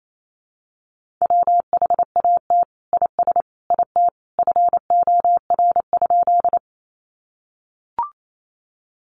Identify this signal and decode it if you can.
Morse code, heard as a keyed tone, about 28 wpm, 700 Hz. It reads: WHAT IS IT FOR?
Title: Sentences from Top 100 Words 28wpm